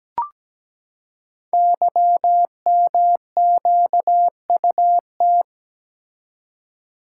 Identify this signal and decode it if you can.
Morse code, heard as a keyed tone, about 17 wpm, 700 Hz. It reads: YMQUT
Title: Code Groups - 5 Letters - 17wpm